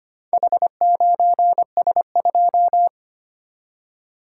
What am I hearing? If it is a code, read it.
Morse code, heard as a keyed tone, about 25 wpm, 690 Hz. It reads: H9S2
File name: practice / 25wpm / Four Letter-Number Combinations - 25wpm